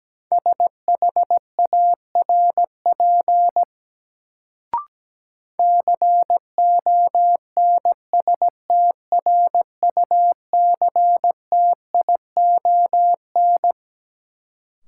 Morse code, heard as a keyed tone, about 17 wpm, 700 Hz. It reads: SHARP CONSTRUCTION
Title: Sentences from Top 1600 Words - New Word Drill - 17wpm